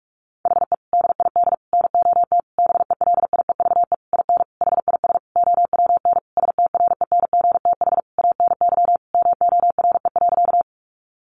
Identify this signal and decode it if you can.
Morse code, heard as a keyed tone, about 44 wpm, 700 Hz. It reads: HE DID NOT BELIEVE IN HIS OWN STRENGTH ANY MORE.
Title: Sentences from Top 1300 Words 45wpm